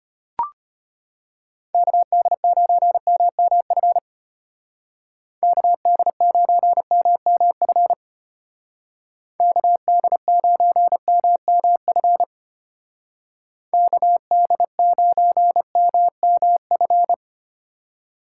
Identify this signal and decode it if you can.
Morse code, heard as a keyed tone, about 30 wpm, 700 Hz. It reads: KD9MMF KD9MMF KD9MMF KD9MMF
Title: Speed Racing - USA 2x3 Call Signs 25wpm